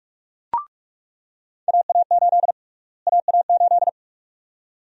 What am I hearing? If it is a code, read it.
Morse code, heard as a keyed tone, about 47 wpm, 700 Hz. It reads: AA8 AA8
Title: USA Call Sign Prefixes - Repeated 2x - 45wpm